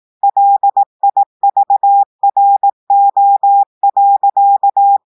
Morse code, heard as a keyed tone, about 18 wpm, 800 Hz. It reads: LIVRO.
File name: audios / portugais / portugais texte en clair 1 vitesse 18 mots minutes